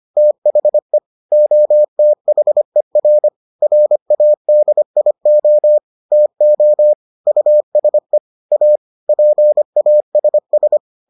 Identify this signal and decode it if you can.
Morse code, heard as a keyed tone, about 25 wpm, 600 Hz. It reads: THE OTHER RADIO TO USE A PASS